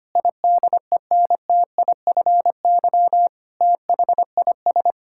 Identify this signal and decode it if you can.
Morse code, heard as a keyed tone, about 25 wpm, 700 Hz. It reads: IDENTIFY THIS
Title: Sentences from Top 1000 Words - Review Part 6 - 25wpm